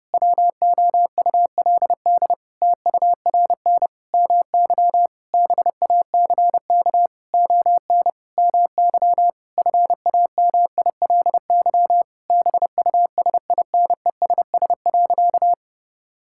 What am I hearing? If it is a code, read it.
Morse code, heard as a keyed tone, about 30 wpm, 700 Hz. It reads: WOULD TURN MY BACK ON MY FAMILY BUSINESS.
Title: Sentences from Top 1000 Words - Review Part 6 - 30wpm